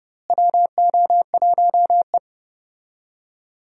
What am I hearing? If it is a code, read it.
Morse code, heard as a keyed tone, about 30 wpm, 700 Hz. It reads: WO1E
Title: USA Call Signs - 4 Characters - Repeated 2x - 30wpm